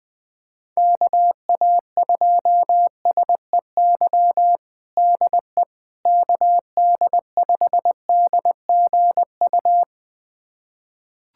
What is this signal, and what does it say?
Morse code, heard as a keyed tone, about 21 wpm, 700 Hz. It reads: KA2SEY DE KD5DGU
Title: USA Callsign DE Callsign 20wpm